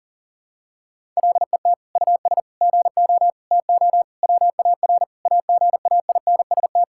Morse code, heard as a keyed tone, about 40 wpm, 700 Hz. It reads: LET US GO TO WAR AGAINST
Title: Sentences from Top 1000 Words - Review Part 5 - 40wpm